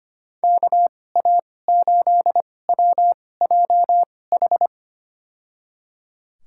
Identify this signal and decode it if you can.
Morse code, heard as a keyed tone, about 25 wpm, spaced as for 18 wpm, 700 Hz. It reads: KA8WJH
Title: USA Call Signs - 6 Characters - 2x Character Spacing - 25wpm